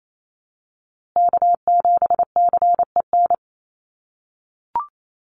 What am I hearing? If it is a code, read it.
Morse code, heard as a keyed tone, about 28 wpm, 700 Hz. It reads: K7CEN
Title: USA Call Signs Repeated 3x 28wpm